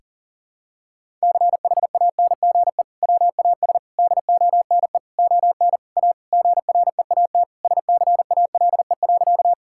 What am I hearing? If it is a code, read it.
Morse code, heard as a keyed tone, about 37 wpm, 700 Hz. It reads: CHANGE WAS DONE ON A GREAT SCALE.
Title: Sentences from Top 1000 Words - Review Part 5 - 40wpm